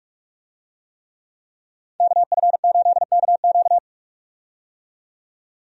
Morse code, heard as a keyed tone, about 47 wpm, 700 Hz. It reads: KF8KQ
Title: USA Call Signs - 5 Characters - 45wpm